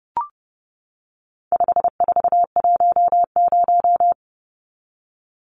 Morse code, heard as a keyed tone, about 30 wpm, 700 Hz. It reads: 5410
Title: Numbers - 4-Digit 30wpm